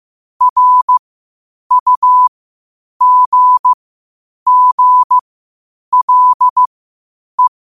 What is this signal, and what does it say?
Morse code, heard as a keyed tone, about 15 wpm, spaced as for 8 wpm, 1000 Hz. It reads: RUGGLE
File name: qotdfiles / 8wpm